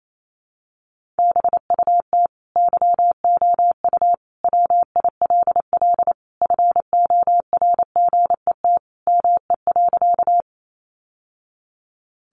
Morse code, heard as a keyed tone, about 28 wpm, 700 Hz. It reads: BUT YOU WILL FORGET ME.
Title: Sentences from Top 1200 Words 28wpm